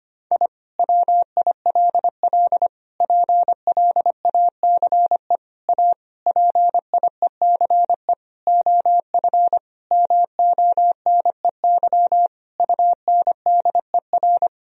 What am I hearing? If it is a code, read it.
Morse code, heard as a keyed tone, about 25 wpm, 700 Hz. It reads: I WILL PLACE A PIECE OF MONEY UNDER
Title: Sentences from Top 1000 Words - Review Part 2 - 25wpm